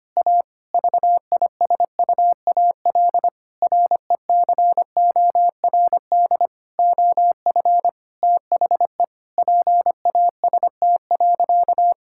Morse code, heard as a keyed tone, about 25 wpm, 700 Hz. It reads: A VISUAL RECORD OF THE PAST.
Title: Sentences from Top 1000 Words - Review Part 6 - 25wpm